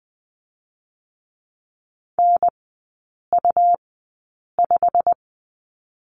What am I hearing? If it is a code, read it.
Morse code, heard as a keyed tone, about 20 wpm, 700 Hz. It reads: N U 5
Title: 100 Lesson 17e - Letter u - Three Characters 20wpm